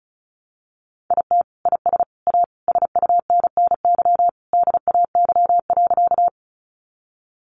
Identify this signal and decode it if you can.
Morse code, heard as a keyed tone, about 34 wpm, 700 Hz. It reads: IT IS A SUNNY DAY.
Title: Easy Phrases 35wpm